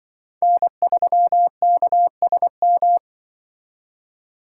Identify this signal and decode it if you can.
Morse code, heard as a keyed tone, about 24 wpm, 700 Hz. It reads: N3KSM
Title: Speed Racing - USA 1x3 Call Signs 20wpm